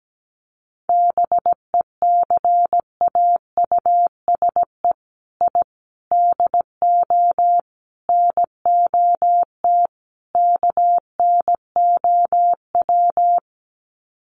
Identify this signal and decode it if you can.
Morse code, heard as a keyed tone, about 17 wpm, 700 Hz. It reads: BECAUSE I DO NOT KNOW
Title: Sets of 5 Words Encore2 17wpm